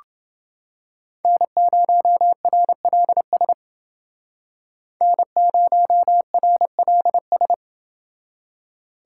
Morse code, heard as a keyed tone, about 28 wpm, 700 Hz. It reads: N0RLS N0RLS